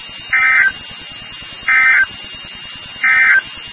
Emergency Alert System (SAME) tones, heard as harsh buzzing bursts. It is heard from 0.2 to 3.5 s.
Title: NOAA Weather Radio WXK36